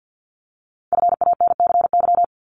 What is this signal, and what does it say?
Morse code, heard as a keyed tone, about 50 wpm, 700 Hz. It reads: FANCY
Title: Multi-letter Phonemes 50wpm